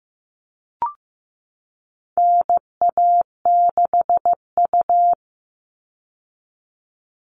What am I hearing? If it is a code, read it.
Morse code, heard as a keyed tone, about 15 wpm, 700 Hz. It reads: NA6U
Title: All Cut Numbers - 4-Digit 15wpm